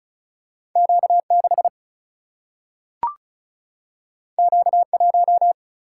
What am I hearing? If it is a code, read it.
Morse code, heard as a keyed tone, about 36 wpm, 700 Hz. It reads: Q 6 Q 1
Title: Single-Letter Single-Number 35wpm